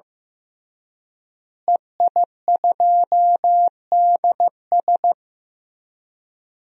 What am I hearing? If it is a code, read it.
Morse code, heard as a keyed tone, about 15 wpm, 700 Hz. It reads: EI2DS
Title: DXCC-100 Call Signs - 5 Characters - Repeated 3x - 15wpm